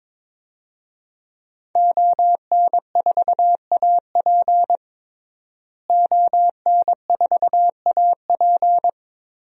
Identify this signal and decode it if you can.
Morse code, heard as a keyed tone, about 22 wpm, 700 Hz. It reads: ON4AP ON4AP
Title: DXCC-100 Call Signs - 5 Characters - Repeated 2x - 22wpm